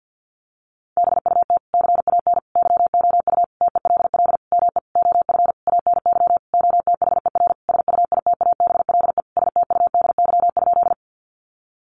Morse code, heard as a keyed tone, about 50 wpm, 700 Hz. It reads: BUT CAN YOU TELL ME OF ANY OTHER SUITABLE STUDY?
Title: Sentences from Top 1400 Words 50wpm